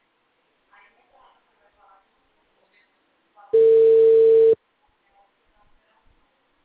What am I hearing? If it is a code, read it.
Ringback tone (China, 450 Hz), with speech.
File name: out- -2025- - - .19731